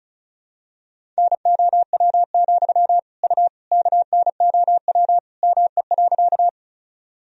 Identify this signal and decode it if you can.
Morse code, heard as a keyed tone, about 34 wpm, 700 Hz. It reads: NOW, U KNOW ME.
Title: Sentences from Top 200 Words and Abbreviations 35wpm